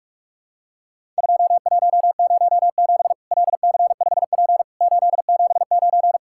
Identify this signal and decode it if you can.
Morse code, heard as a keyed tone, about 46 wpm, 700 Hz. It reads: 2107 LC5P 879
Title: Worked All Europe DX Contest - QTC Exchange - 45wpm